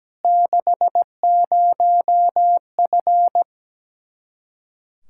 Morse code, heard as a keyed tone, about 17 wpm, 700 Hz. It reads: 60F
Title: Practice QSO - Exchange 2 - Info - 17wpm